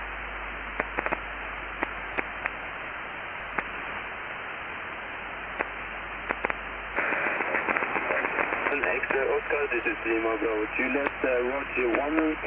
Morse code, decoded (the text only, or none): none